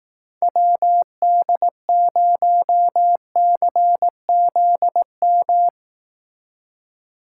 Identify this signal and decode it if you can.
Morse code, heard as a keyed tone, about 18 wpm, 700 Hz. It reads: WD0CZM
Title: Speed Racing - USA 2x3 Call Signs 15wpm